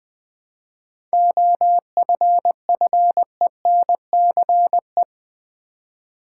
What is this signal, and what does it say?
Morse code, heard as a keyed tone, about 21 wpm, 700 Hz. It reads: OFFENCE
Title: 7 Letter Words - First 3 Letters Given - 20wpm